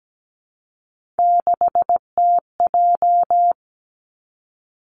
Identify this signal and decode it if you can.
Morse code, heard as a keyed tone, about 17 wpm, 700 Hz. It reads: 6TJ
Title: Three Challenging Letter-Number Combinations 17wpm